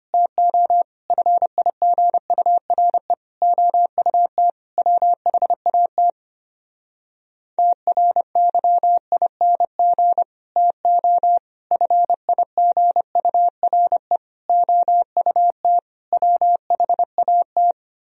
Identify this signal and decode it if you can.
Morse code, heard as a keyed tone, about 24 wpm, 700 Hz. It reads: TO FIGURE OUT WHAT TRYING TO FIGURE OUT WHAT
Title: Speed-Racing - Sets of 5 Words 25wpm